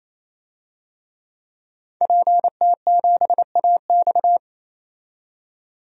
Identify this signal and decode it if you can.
Morse code, heard as a keyed tone, about 28 wpm, 700 Hz. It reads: PT7AX
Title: DXCC-100 Call Signs - 5 Characters - 28wpm